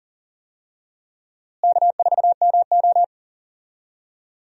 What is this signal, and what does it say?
Morse code, heard as a keyed tone, about 39 wpm, 700 Hz. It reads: K4MO
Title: USA Call Signs - 4 Characters - Repeated 3x - 40wpm